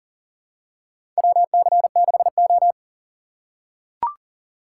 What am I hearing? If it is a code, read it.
Morse code, heard as a keyed tone, about 40 wpm, 700 Hz. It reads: WC6O
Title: USA Call Signs - 4 Characters - 2x Character Spacing - 40wpm